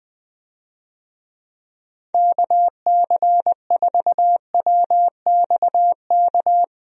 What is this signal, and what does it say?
Morse code, heard as a keyed tone, about 21 wpm, 700 Hz. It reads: KC4WXK
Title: USA Call Signs - 6 Characters - Repeated 3x - 20wpm